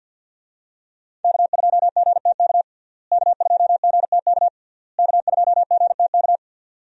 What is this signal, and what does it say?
Morse code, heard as a keyed tone, about 48 wpm, 700 Hz. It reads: K2GTK K2GTK K2GTK
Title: USA Call Signs - 5 Characters - Repeated 3x - 50wpm